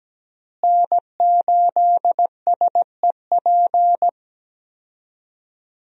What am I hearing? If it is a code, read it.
Morse code, heard as a keyed tone, about 17 wpm, 700 Hz. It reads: N8SEP
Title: USA Call Signs - 5 Characters - Repeated 3x - 17wpm